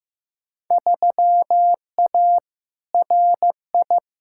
Morse code, heard as a keyed tone, about 15 wpm, 700 Hz. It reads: 3A RI
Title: Field Day Exchanges - 15wpm